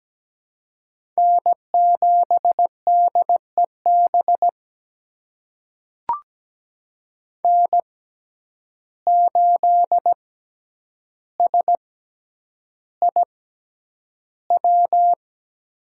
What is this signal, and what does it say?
Morse code, heard as a keyed tone, about 17 wpm, spaced as for 9 wpm, 700 Hz. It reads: N7DEB N8SIW
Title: USA Call Signs - 5 Characters - 6x Character Spacing - 17wpm